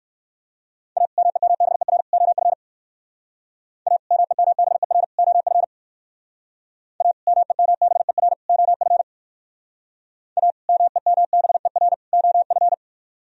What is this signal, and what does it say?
Morse code, heard as a keyed tone, about 61 wpm, 700 Hz. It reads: A MEMBER OF A MEMBER OF A MEMBER OF A MEMBER OF